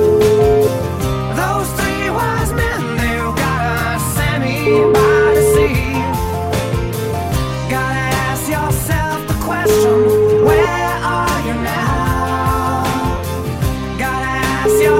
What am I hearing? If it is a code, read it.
Ringback tone (Europe, 425 Hz), with noise.